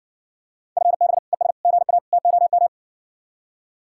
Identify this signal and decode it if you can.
Morse code, heard as a keyed tone, about 60 wpm, 700 Hz. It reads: FB ES GA TOM